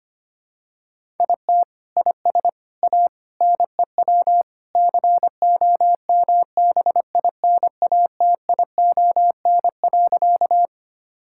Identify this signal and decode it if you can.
Morse code, heard as a keyed tone, about 25 wpm, 700 Hz. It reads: IT IS A NEW COMBINATION.
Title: Sentences from Top 1000 Words - Review Part 6 - 25wpm